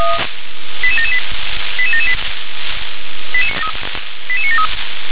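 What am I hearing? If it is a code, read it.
Five-tone selective call (ZVEI), heard as quick runs of five tones, tones of about 80 ms. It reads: 88778 88778 88973
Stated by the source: Five-tone ZVEI selective call received by SDR.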